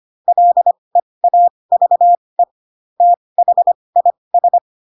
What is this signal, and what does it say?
Morse code, heard as a keyed tone, about 25 wpm, spaced as for 20 wpm, 700 Hz. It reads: LEAVE THIS
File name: Pinocchio / Pinocchio ch4 15 25 ik1hge